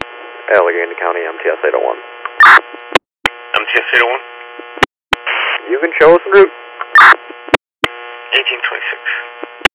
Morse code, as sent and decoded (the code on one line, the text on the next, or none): none